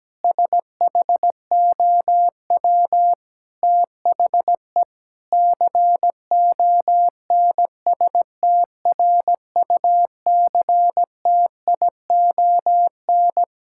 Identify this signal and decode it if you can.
Morse code, heard as a keyed tone, about 17 wpm, 700 Hz. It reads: SHOW THE CONSTRUCTION
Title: Sentences from Top 1600 Words 17wpm